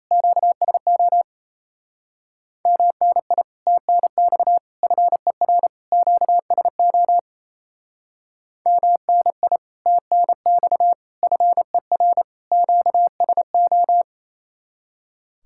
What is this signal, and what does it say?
Morse code, heard as a keyed tone, about 34 wpm, 700 Hz. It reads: QSO MNI TNX FER QSO MNI TNX FER QSO
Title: Speed-Racing Common QSO Phrases Encore - 28wpm